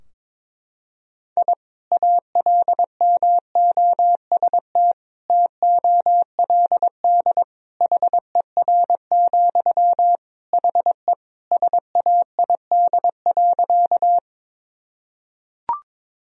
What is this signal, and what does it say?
Morse code, heard as a keyed tone, about 22 wpm, 700 Hz. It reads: I ALMOST TOLD HER, HE SAID.
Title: Sentences from Top 1000 Words - Review Part 4 - 22wpm